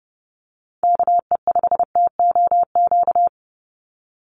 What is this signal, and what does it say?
Morse code, heard as a keyed tone, about 30 wpm, 700 Hz. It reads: KE5TOQ
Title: USA Call Signs 30wpm